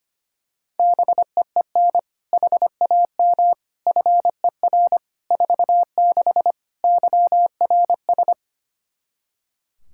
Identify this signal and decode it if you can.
Morse code, heard as a keyed tone, about 25 wpm, 700 Hz. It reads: BEEN HAM FER 46 YRS
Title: Practice QSO - Exchange 3 - More Info - 25wpm